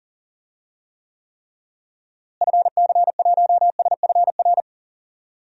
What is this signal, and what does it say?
Morse code, heard as a keyed tone, about 40 wpm, 700 Hz. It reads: FC1SFR
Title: DXCC-100 Call Signs - 6 Characters - Repeated 3x - 40wpm